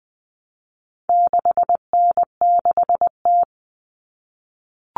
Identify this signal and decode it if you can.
Morse code, heard as a keyed tone, about 20 wpm, 700 Hz. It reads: 6N6T
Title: All Cut Numbers - 4-Digit 20wpm